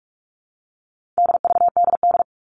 Morse code, heard as a keyed tone, about 44 wpm, 700 Hz. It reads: DVDD
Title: All Cut Numbers - 4-Digit 45wpm